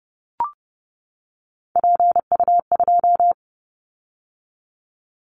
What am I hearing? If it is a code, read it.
Morse code, heard as a keyed tone, about 30 wpm, 700 Hz. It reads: PU2
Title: Callsign Prefixes for Top 100 DXCC - 30wpm